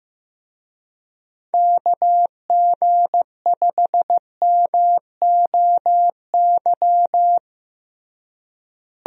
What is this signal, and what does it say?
Morse code, heard as a keyed tone, about 15 wpm, 700 Hz. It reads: KG5MOY
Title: USA Call Signs - 6 Characters - Repeated 2x - 15wpm